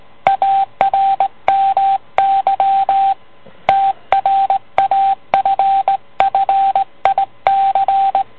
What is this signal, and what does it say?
Morse code, heard as a keyed tone, about 18 wpm, 780 Hz, with radio noise. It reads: ARMY TRAFFIC